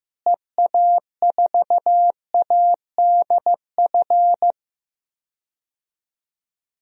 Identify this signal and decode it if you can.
Morse code, heard as a keyed tone, about 15 wpm, 700 Hz. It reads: EA4ADF
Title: DXCC-100 Call Signs - 6 Characters - Repeated 2x - 15wpm